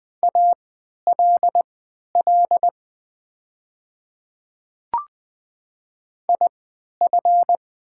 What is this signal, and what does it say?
Morse code, heard as a keyed tone, about 20 wpm, spaced as for 11 wpm, 700 Hz. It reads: ALL IF
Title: 3x Character Spacing - Top 200 Words - 20wpm